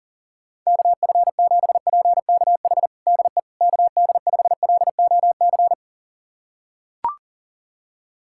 Morse code, heard as a keyed tone, about 37 wpm, 700 Hz. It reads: KF7PKH DE KD5LOC
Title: USA Callsign DE Callsign 40wpm